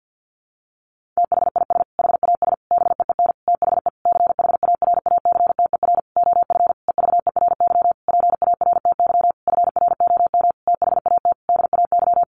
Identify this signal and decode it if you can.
Morse code, heard as a keyed tone, about 50 wpm, 700 Hz. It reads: THIS HAS BEEN THE CHARACTER OF EVERY PARTY FROM THAT DAY
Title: Sentences from Top 1400 Words 50wpm